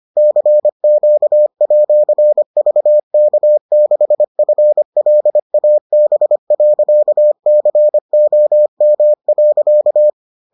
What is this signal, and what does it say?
Morse code, heard as a keyed tone, about 25 wpm, 600 Hz. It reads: CQ@VK6FLAB.COM.